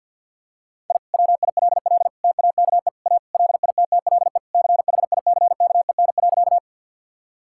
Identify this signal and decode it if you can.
Morse code, heard as a keyed tone, about 46 wpm, 700 Hz. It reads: I WILL TAKE A LITTLE CHICKEN.
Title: Sentences from Top 1000 Words - Review Part 5 - 50wpm